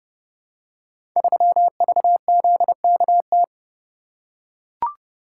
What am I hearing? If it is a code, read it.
Morse code, heard as a keyed tone, about 30 wpm, 700 Hz. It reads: 3VZKT
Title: Code Groups - 5 Letter:Numbers - 30wpm